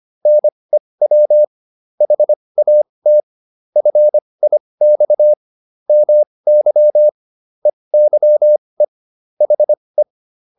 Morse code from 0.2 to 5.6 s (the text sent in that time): NEW HAT FIX